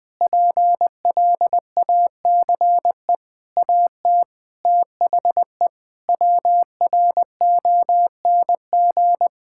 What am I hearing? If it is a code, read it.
Morse code, heard as a keyed tone, about 21 wpm, 700 Hz. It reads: PLACE AT THE WRONG
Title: Sets of 8 Words - 20wpm